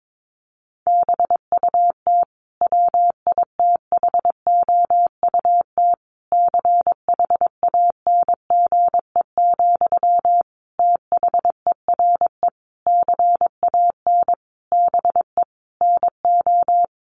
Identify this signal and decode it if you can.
Morse code, heard as a keyed tone, about 22 wpm, 700 Hz. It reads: BUT WITHOUT CHANGE, THERE CAN BE NO
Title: Sentences from Top 1400 Words 22wpm